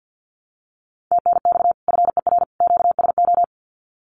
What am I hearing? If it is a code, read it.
Morse code, heard as a keyed tone, about 48 wpm, 700 Hz. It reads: TNX FER QSO
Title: Common QSO Phrases 50wpm